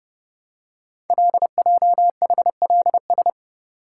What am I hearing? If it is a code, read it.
Morse code, heard as a keyed tone, about 30 wpm, 700 Hz. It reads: LJHLS
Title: Code Groups - 5 Letters - 30wpm